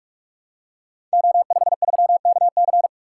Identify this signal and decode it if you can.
Morse code, heard as a keyed tone, about 44 wpm, 700 Hz. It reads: O53KC
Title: Code Groups - 5 Letter:Numbers - 45wpm